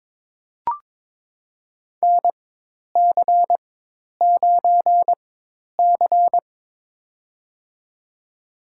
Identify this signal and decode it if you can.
Morse code, heard as a keyed tone, about 22 wpm, spaced as for 10 wpm, 700 Hz. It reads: NC9C
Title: USA Call Signs - 4x Character Spacing - 22wpm